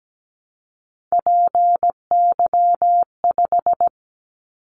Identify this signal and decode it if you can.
Morse code, heard as a keyed tone, about 17 wpm, 700 Hz. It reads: PY5
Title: Callsign Prefixes for Top 100 DXCC - 17wpm